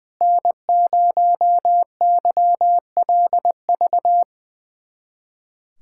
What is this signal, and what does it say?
Morse code, heard as a keyed tone, about 20 wpm, 700 Hz. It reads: N0YLV
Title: USA Callsign DE Callsign - 5 Characters - 20wpm